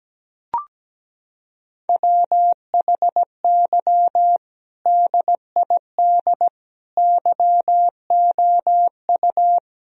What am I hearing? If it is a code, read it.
Morse code, heard as a keyed tone, about 17 wpm, 700 Hz. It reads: WHY DID YOU
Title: Sentences from Top 1000 Words - Review Part 1 - 17wpm